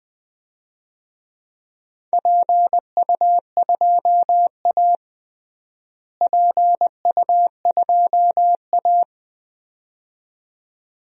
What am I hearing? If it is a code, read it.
Morse code, heard as a keyed tone, about 21 wpm, 700 Hz. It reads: PU2A PU2A
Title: DXCC-100 Call Signs - 4 Characters - Repeated 2x - 20wpm